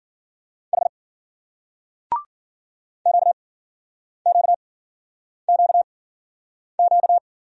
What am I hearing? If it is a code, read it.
Morse code, heard as a keyed tone, about 49 wpm, spaced as for 9 wpm, 700 Hz. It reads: H QQQQ
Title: Speed-Racing Single Letters 40wpm